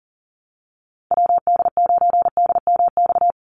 Morse code, heard as a keyed tone, about 40 wpm, 700 Hz. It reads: WD9DMX
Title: USA Call Signs 40wpm